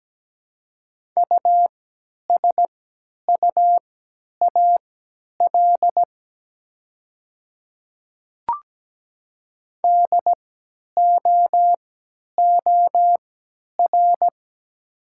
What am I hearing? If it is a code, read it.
Morse code, heard as a keyed tone, about 17 wpm, spaced as for 10 wpm, 700 Hz. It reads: USUAL DOOR
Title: 3x Character Spacing - Top 500 Words - 17wpm